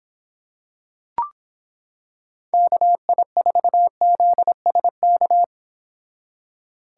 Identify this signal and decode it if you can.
Morse code, heard as a keyed tone, about 26 wpm, 700 Hz. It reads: KI4ZSK